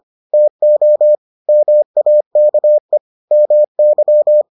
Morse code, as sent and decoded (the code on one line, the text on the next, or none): - --- / -- .- -.- . / -- -.--
TO MAKE MY